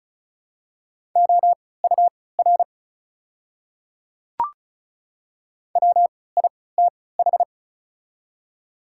Morse code, heard as a keyed tone, about 36 wpm, spaced as for 20 wpm, 700 Hz. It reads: OUR WITH
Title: 3x Character Spacing - Top 300 Words - 35wpm